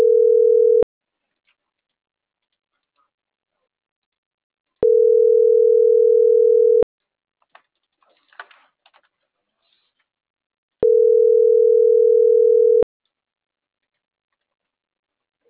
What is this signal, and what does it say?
Ringback tone (North America), 440 + 480 Hz.